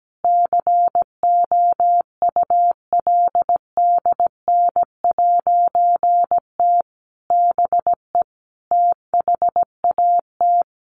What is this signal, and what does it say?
Morse code, heard as a keyed tone, about 17 wpm, 700 Hz. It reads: COULDN'T BE THAT
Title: A Sound Of Thunder 17wpm